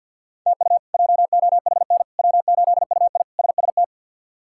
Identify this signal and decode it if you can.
Morse code, heard as a keyed tone, about 47 wpm, 700 Hz. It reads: TU JOHN W8UI SST
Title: K1USN SST QSO Practice - 50wpm